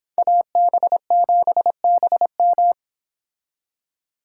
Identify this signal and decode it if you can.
Morse code, heard as a keyed tone, about 30 wpm, 700 Hz. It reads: AB7BM